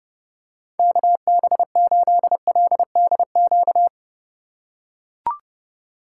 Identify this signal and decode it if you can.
Morse code, heard as a keyed tone, about 29 wpm, 700 Hz. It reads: KB8LDQ